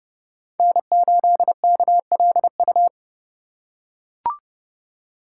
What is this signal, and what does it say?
Morse code, heard as a keyed tone, about 32 wpm, 700 Hz. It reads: N8KLU
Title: USA Call Signs - 2x Character Spacing - 30wpm